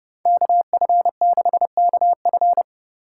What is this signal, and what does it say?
Morse code, heard as a keyed tone, about 28 wpm, 700 Hz. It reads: KF6KF